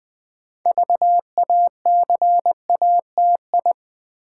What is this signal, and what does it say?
Morse code, heard as a keyed tone, about 20 wpm, 700 Hz. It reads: VACATI
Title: Build A Word - 3 Letters At A Time - 8-Letter Words - 20wpm